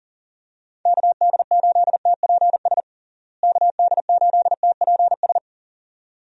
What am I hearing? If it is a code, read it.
Morse code, heard as a keyed tone, about 40 wpm, 700 Hz. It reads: KD8TPS KD8TPS
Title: USA Call Signs - 6 Characters - Repeated 2x - 40wpm